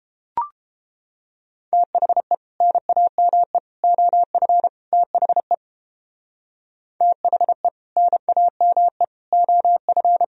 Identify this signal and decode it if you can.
Morse code, heard as a keyed tone, about 33 wpm, 700 Hz. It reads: THE NAME OF THE THE NAME OF